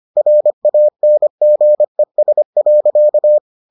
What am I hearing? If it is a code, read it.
Morse code, heard as a keyed tone, about 25 wpm, 600 Hz. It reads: RANGES.